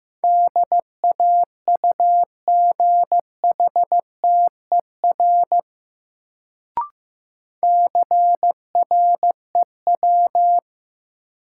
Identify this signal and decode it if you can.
Morse code, heard as a keyed tone, about 16 wpm, 700 Hz. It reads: DAUGHTER CREW
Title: Sentences from Top 1500 Words - New Word Drill - 15wpm